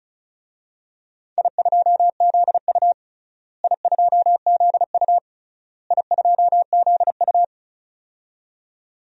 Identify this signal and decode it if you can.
Morse code, heard as a keyed tone, about 34 wpm, 700 Hz. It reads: I2ZU I2ZU I2ZU
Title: DXCC-100 Call Signs - 4 Characters - Repeated 3x - 35wpm